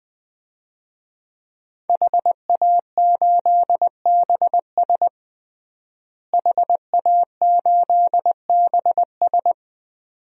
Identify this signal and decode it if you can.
Morse code, heard as a keyed tone, about 20 wpm, 700 Hz. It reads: HA8BS HA8BS
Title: DXCC-100 Call Signs - 5 Characters - Repeated 2x - 20wpm